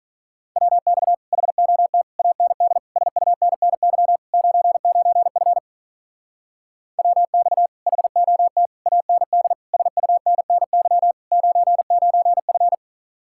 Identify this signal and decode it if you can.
Morse code, heard as a keyed tone, about 44 wpm, 700 Hz. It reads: WX HOT AND SUNNY 99F WX HOT AND SUNNY 99F
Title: Speed-Racing Common QSO Phrases - 35wpm